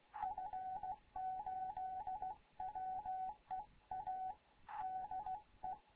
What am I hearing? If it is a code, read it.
Morse code, heard as a keyed tone, about 16 wpm, 750 Hz, with radio noise. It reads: F8WEA DE
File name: Morze-HA2MN / hangp / 2 11 GYAKORLAT